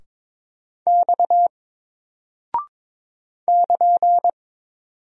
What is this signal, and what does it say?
Morse code, heard as a keyed tone, about 22 wpm, 700 Hz. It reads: X <KN>
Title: Single Character (Letter:Number:Punctuation:Prosign) Alt 22wpm